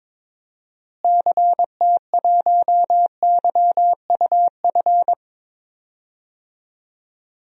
Morse code, heard as a keyed tone, about 21 wpm, 700 Hz. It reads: CT1YUF
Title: DXCC-100 Call Signs - 6 Characters - Repeated 3x - 22wpm